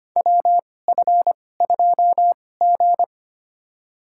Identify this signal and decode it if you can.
Morse code, heard as a keyed tone, about 25 wpm, spaced as for 18 wpm, 700 Hz. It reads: WF2G
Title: USA Call Signs - 4 Characters - 2x Character Spacing - 25wpm